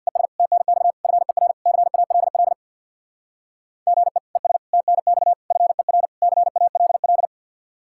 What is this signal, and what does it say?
Morse code, heard as a keyed tone, about 67 wpm, 700 Hz. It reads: ES TNX FER CALL GE ES TNX FER CALL